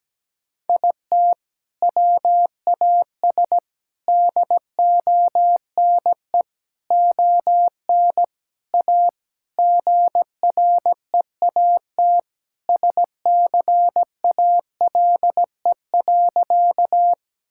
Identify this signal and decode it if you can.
Morse code, heard as a keyed tone, about 17 wpm, 700 Hz. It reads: IT WAS DONE ON A GREAT SCALE.
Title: Sentences from Top 1000 Words - Review Part 5 - 17wpm